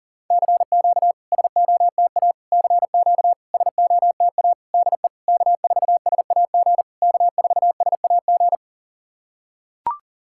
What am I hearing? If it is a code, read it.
Morse code, heard as a keyed tone, about 41 wpm, 690 Hz. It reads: CQ SOTA CQ SOTA DE K4SAG K4SAG
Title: SOTA QSO Practice 40wpm